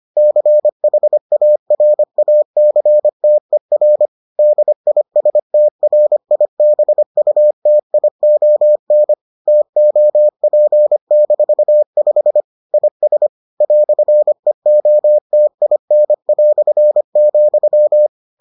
Morse code, heard as a keyed tone, about 25 wpm, 600 Hz. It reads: CHARACTER DISTRIBUTION TOP-5 IS "EOTIN",